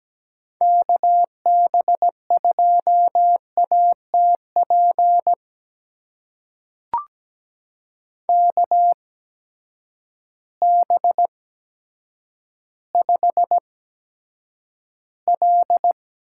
Morse code, heard as a keyed tone, about 17 wpm, spaced as for 9 wpm, 700 Hz. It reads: KB2ATP KB5L
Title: USA Call Signs - 8x Character Spacing - 17wpm